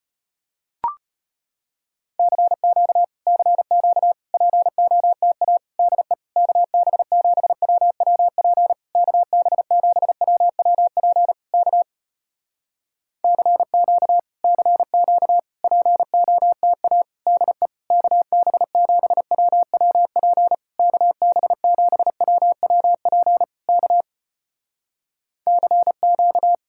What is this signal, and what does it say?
Morse code, heard as a keyed tone, about 36 wpm, 700 Hz. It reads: CQ CQ POTA DE KB7WWP KB7WWP K CQ CQ POTA DE KB7WWP KB7WWP K CQ